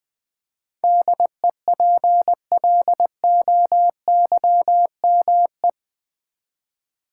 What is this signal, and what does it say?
Morse code, heard as a keyed tone, about 20 wpm, 700 Hz. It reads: DEPLOYME
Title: Build A Word - 4 Letters At A Time - 10-Letter Words - 20wpm